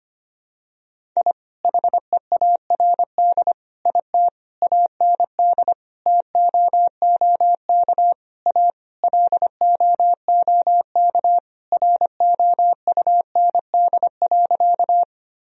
Morse code, heard as a keyed tone, about 25 wpm, 700 Hz. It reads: I HEARD IT AND TOOK A LOOK ROUND.
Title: Sentences from Top 1000 Words - Review Part 1 - 25wpm